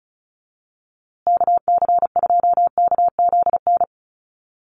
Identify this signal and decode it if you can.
Morse code, heard as a keyed tone, about 34 wpm, 700 Hz. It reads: KC2KZN
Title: USA Call Signs Repeated 3x 35wpm